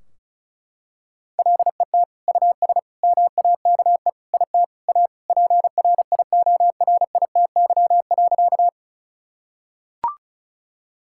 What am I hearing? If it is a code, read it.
Morse code, heard as a keyed tone, about 36 wpm, 700 Hz. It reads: LET US MAKE IT A PRIORITY.
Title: Sentences from Top 1600 Words 35wpm